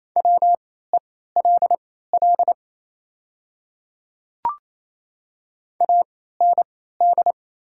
Morse code, heard as a keyed tone, about 28 wpm, spaced as for 16 wpm, 700 Hz. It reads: WELL AND
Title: 3x Character Spacing - Top 300 Words - 28wpm